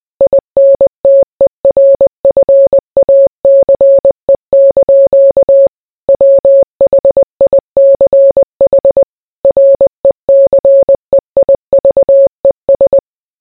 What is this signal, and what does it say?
Morse code, heard as a keyed tone, about 20 wpm, 550 Hz. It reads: INTERFACE) WHICH RECEIVES